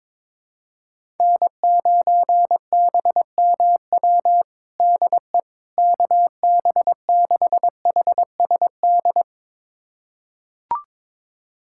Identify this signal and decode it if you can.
Morse code, heard as a keyed tone, about 21 wpm, 700 Hz. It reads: N9BMW DE KB6HSD
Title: USA Callsign DE Callsign 22wpm